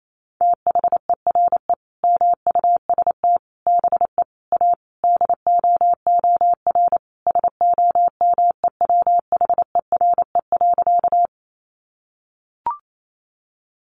Morse code, heard as a keyed tone, about 28 wpm, 700 Hz. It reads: THERE MUST BE A DOOR SOMEWHERE.
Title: Sentences from Top 1400 Words 28wpm